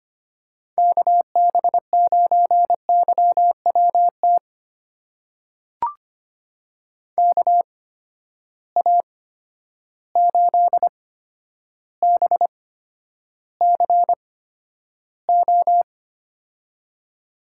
Morse code, heard as a keyed tone, about 25 wpm, spaced as for 11 wpm, 700 Hz. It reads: KB9YWT KA8BCO
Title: USA Call Signs - 6 Characters - 8x Character Spacing - 25wpm